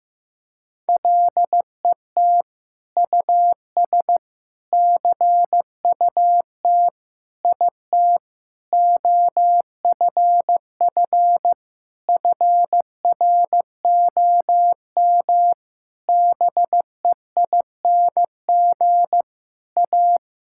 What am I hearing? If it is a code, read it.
Morse code, heard as a keyed tone, about 15 wpm, 700 Hz. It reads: LET US CUT IT OFF FROM BEING A
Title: Sentences from Top 1000 Words - Review Part 6 - 15wpm